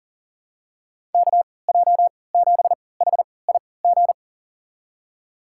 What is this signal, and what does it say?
Morse code, heard as a keyed tone, about 40 wpm, spaced as for 23 wpm, 700 Hz. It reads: KJ7HIG
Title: USA Call Signs - 6 Characters - 3x Character Spacing - 40wpm